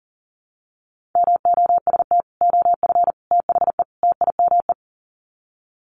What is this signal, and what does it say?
Morse code, heard as a keyed tone, about 40 wpm, 700 Hz. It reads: MOST OF THE TIME
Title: Sets of 4 Words 40wpm